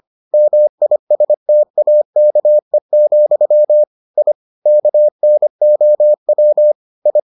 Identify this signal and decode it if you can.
Morse code, heard as a keyed tone, about 25 wpm, 600 Hz. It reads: MISTAKE, I KNOW I